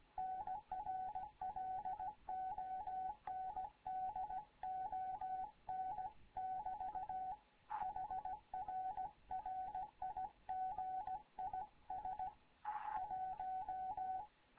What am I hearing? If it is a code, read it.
Morse code, heard as a keyed tone, about 17 wpm, 750 Hz. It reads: NR LONDON <BT> HR RIG IS 1
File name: Morze-HA2MN / hangp / 2 11 GYAKORLAT